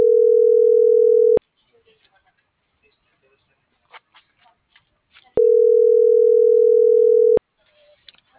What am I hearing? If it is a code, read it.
Ringback tone (North America), 440 + 480 Hz.